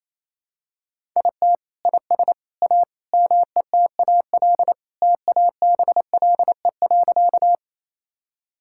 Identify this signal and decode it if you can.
Morse code, heard as a keyed tone, about 28 wpm, 700 Hz. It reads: IT IS A METAL TABLE.
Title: Sentences from Top 1000 Words - Review Part 6 - 28wpm